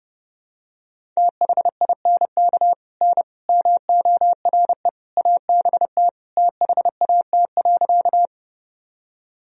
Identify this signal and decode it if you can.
Morse code, heard as a keyed tone, about 30 wpm, 700 Hz. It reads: THINK N MORE ABT THAT.
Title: Sentences from Top 200 Words and Abbreviations 30wpm